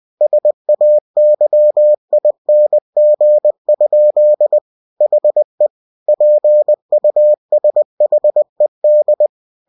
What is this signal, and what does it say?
Morse code, heard as a keyed tone, about 20 wpm, 600 Hz. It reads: SAYING? HE PUSHED
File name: cw-books / winnie-the-pooh / winnie-20wpm-page-0054